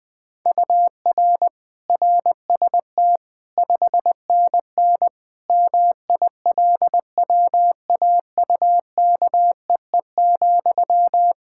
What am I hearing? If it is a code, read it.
Morse code, heard as a keyed tone, about 21 wpm, 700 Hz. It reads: UR RST 5NN MILWAUKEE,
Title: K1USN SST QSO Practice 20wpm